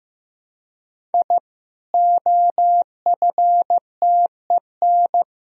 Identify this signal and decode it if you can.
Morse code, heard as a keyed tone, about 15 wpm, 700 Hz. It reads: I OFTEN
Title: Sentences from Top 1000 Words - Review Part 4 - 15wpm